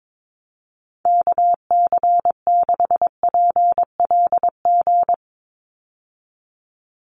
Morse code, heard as a keyed tone, about 22 wpm, 700 Hz. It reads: KC6PLG
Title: USA Call Signs 22wpm